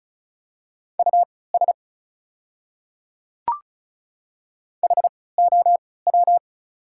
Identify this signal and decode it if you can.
Morse code, heard as a keyed tone, about 34 wpm, spaced as for 20 wpm, 700 Hz. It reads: US HOW
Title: 3x Character Spacing - Top 200 Words - 35wpm